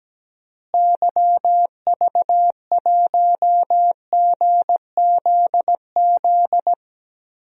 Morse code, heard as a keyed tone, about 17 wpm, 700 Hz. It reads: YV1GZZ
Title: DXCC-100 Call Signs - 6 Characters - Repeated 2x - 17wpm